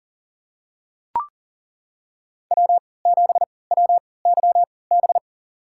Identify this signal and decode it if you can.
Morse code, heard as a keyed tone, about 40 wpm, spaced as for 23 wpm, 700 Hz. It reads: W7WYB
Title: USA Call Signs - 5 Characters - 3x Character Spacing - 40wpm